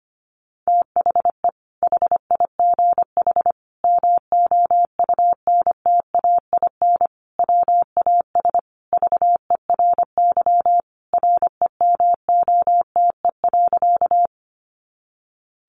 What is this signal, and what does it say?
Morse code, heard as a keyed tone, about 25 wpm, 700 Hz. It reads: THE HIGH MOUNTAIN WAS VERY REMOTE.
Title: Sentences from Top 1300 Words 25wpm